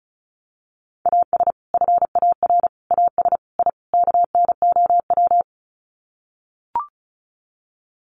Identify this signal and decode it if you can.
Morse code, heard as a keyed tone, about 34 wpm, 700 Hz. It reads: AS FAR AS I KNOW
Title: Sets of 5 Words 35wpm